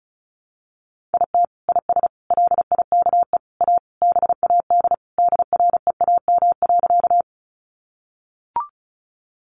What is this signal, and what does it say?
Morse code, heard as a keyed tone, about 34 wpm, 700 Hz. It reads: IT IS LIKE A BAD DREAM.
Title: Sentences from Top 1500 Words 35wpm